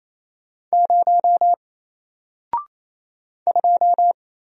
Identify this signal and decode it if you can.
Morse code, heard as a keyed tone, about 28 wpm, 700 Hz. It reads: 0 2
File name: Single-Letter-Number-28wpm